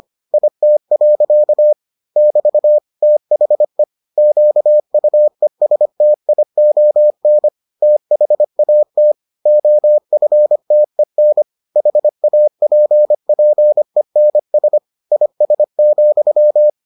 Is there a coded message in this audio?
Yes: Morse code from 0.3 to 16.7 s, heard as a keyed tone, about 25 wpm, 600 Hz. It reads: IT. <BT> THE QUESTION THAT OFTEN HAPPENS IS,